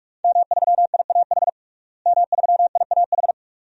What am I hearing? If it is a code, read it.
Morse code, heard as a keyed tone, about 46 wpm, 700 Hz. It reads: M3IAH M3IAH
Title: DXCC-100 Call Signs - 5 Characters - Repeated 2x - 45wpm